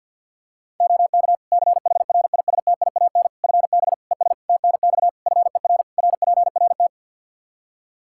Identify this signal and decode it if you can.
Morse code, heard as a keyed tone, about 52 wpm, 700 Hz. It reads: OK CHRISTIAN FB ES TNX FER RPRT
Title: Practice QSO - Exchange 2 - Info - 50wpm